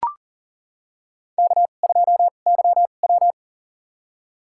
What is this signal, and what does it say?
Morse code, heard as a keyed tone, about 40 wpm, spaced as for 29 wpm, 690 Hz. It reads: K2YW
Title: USA Call Signs - 4 Characters - 2x Character Spacing - 40wpm